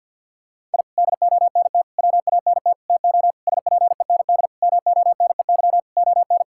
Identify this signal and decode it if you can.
Morse code, heard as a keyed tone, about 48 wpm, 700 Hz. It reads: I DONT WANT TO SPEND MONEY ON
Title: Sentences from Top 1000 Words - Review Part 5 - 50wpm